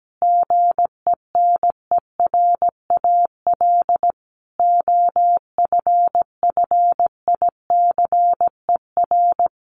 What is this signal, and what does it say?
Morse code, heard as a keyed tone, about 17 wpm, 700 Hz. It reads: GENERAL OFFICER
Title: Sentences from Top 1200 Words 17wpm